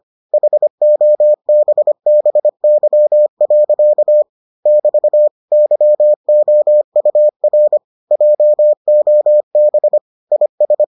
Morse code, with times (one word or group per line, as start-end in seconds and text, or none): 0.3-4.2 HOBBY.
4.7-5.3 <BT>
5.5-7.8 YOUR
8.1-10.0 JOB
10.3-10.8 IS